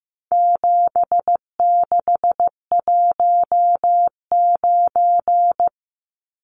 Morse code, heard as a keyed tone, about 15 wpm, 700 Hz. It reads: 7619
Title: Numbers - 4-Digit 15wpm